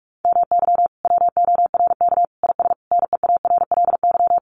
Morse code, heard as a keyed tone, about 44 wpm, 700 Hz. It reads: MY WORK IS NEARLY
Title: Sentences from Top 1300 Words 45wpm